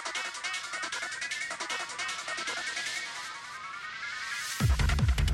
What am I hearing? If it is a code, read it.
Emergency Alert System (SAME) tones, heard as harsh buzzing bursts. It is heard from 0.7 to 4.5 s.